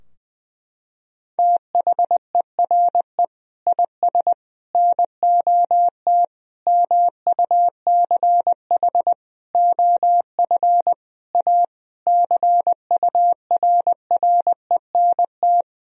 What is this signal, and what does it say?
Morse code, heard as a keyed tone, about 21 wpm, 700 Hz. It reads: THERE IS NOT MUCH OF A CURRENT
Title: Sentences from Top 1000 Words - Review Part 4 - 20wpm